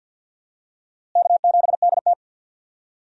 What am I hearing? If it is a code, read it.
Morse code, heard as a keyed tone, about 52 wpm, 700 Hz. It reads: K7DT
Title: USA Call Signs - 4 Characters - Repeated 2x - 50wpm